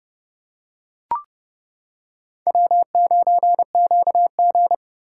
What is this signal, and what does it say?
Morse code, heard as a keyed tone, about 29 wpm, 700 Hz. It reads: W9QG